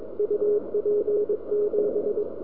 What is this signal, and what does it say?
Morse code, heard as a keyed tone, about 21 wpm, 420 Hz, with radio noise. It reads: UP7
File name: Recordings / vp6dx 160 25 2 08 0725